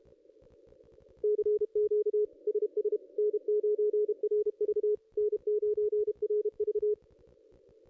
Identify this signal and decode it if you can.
Morse code, heard as a keyed tone, about 30 wpm, 440 Hz. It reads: CQ SS N9RV N9RV
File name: contests / ss-cw-k5tr-op-n5rz -2200